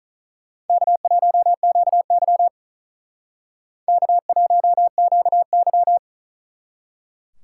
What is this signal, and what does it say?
Morse code, heard as a keyed tone, about 45 wpm, 700 Hz. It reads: K1QY K1QY